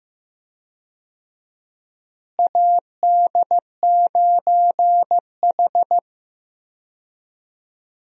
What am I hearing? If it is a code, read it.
Morse code, heard as a keyed tone, about 15 wpm, 700 Hz. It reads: AD9H
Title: USA Call Signs - 4 Characters - Repeated 3x - 15wpm